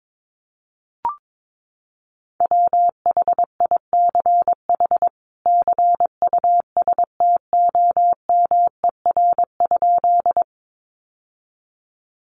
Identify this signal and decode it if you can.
Morse code, heard as a keyed tone, about 22 wpm, 700 Hz. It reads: WHICH CUSTOMER?
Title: Sentences from Top 1100 Words 22wpm